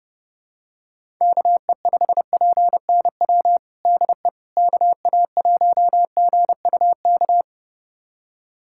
Morse code, heard as a keyed tone, about 30 wpm, 700 Hz. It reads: KE5PNW DE KA1GUK
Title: USA Callsign DE Callsign - 6 Characters - 30wpm